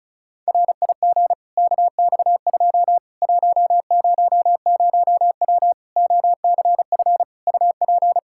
Morse code, heard as a keyed tone, about 36 wpm, 700 Hz. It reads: RIG KX2 100W OCF UP
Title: Practice QSO - Exchange 2 - Info - 35wpm